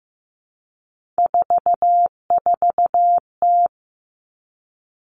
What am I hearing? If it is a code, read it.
Morse code, heard as a keyed tone, about 15 wpm, 700 Hz. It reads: 44T
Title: Three Challenging Letter-Number Combinations 15wpm